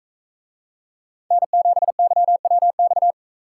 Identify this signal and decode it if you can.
Morse code, heard as a keyed tone, about 44 wpm, 700 Hz. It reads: N7YWX